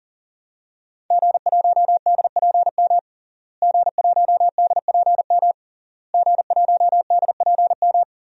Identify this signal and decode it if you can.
Morse code, heard as a keyed tone, about 40 wpm, 700 Hz. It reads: G1DPM G1DPM G1DPM
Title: DXCC-100 Call Signs - 5 Characters - Repeated 3x - 40wpm